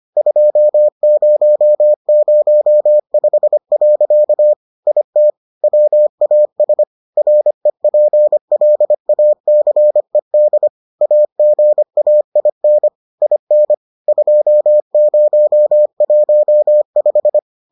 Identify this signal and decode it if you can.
Morse code, heard as a keyed tone, about 25 wpm, 600 Hz. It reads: 2005. IT WAS REPLACED AGAIN IN 2015